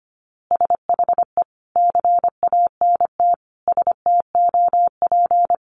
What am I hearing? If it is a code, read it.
Morse code, heard as a keyed tone, about 25 wpm, 700 Hz. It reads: SHE CANT STOP
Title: Easy Phrases 25wpm